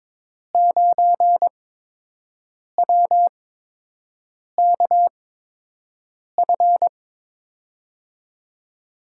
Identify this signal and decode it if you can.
Morse code, heard as a keyed tone, about 22 wpm, spaced as for 6 wpm, 700 Hz. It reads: 9WKF
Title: USA Call Signs - 5 Characters - 8x Character Spacing - 22wpm